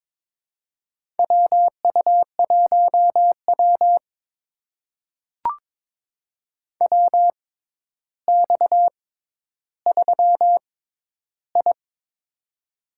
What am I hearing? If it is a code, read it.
Morse code, heard as a keyed tone, about 23 wpm, spaced as for 11 wpm, 700 Hz. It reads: WU1W WX3I
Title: USA Call Signs - 4 Characters - 6x Character Spacing - 22wpm